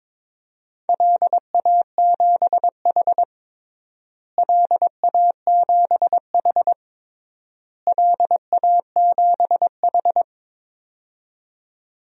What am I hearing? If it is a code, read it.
Morse code, heard as a keyed tone, about 22 wpm, 700 Hz. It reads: LA7H LA7H LA7H
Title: DXCC-100 Call Signs - 4 Characters - Repeated 3x - 22wpm